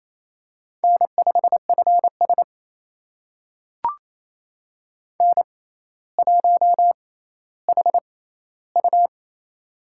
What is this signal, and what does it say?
Morse code, heard as a keyed tone, about 28 wpm, spaced as for 14 wpm, 700 Hz. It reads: N5FS N1HU
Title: USA Call Signs - 4 Characters - 6x Character Spacing - 28wpm